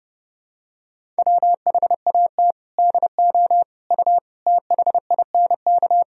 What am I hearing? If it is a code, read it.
Morse code, heard as a keyed tone, about 32 wpm, 700 Hz. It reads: WHAT DO U THINK
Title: Sentences from Top 300 Words and Abbreviations 30wpm